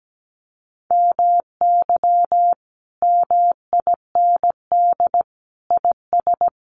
Morse code, heard as a keyed tone, about 17 wpm, 700 Hz. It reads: MY MIND IS
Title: Sentences from Top 1100 Words 17wpm